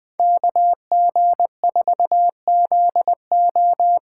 Morse code, heard as a keyed tone, about 20 wpm, 700 Hz. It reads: KG4ZO
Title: USA Call Signs - 6 Characters - 8x Character Spacing - 20wpm